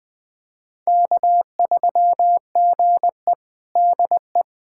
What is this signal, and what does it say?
Morse code, heard as a keyed tone, about 20 wpm, 700 Hz. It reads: K3GE DE
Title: USA Callsign DE Callsign - 4 Characters - 20wpm